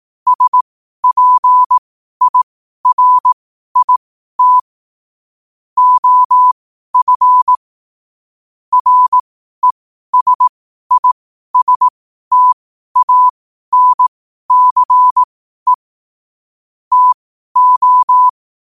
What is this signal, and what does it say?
Morse code, heard as a keyed tone, about 18 wpm, spaced as for 12 wpm, 1000 Hz. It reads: SPIRIT OF RESISTANCE TO
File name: qotdfiles / 12wpm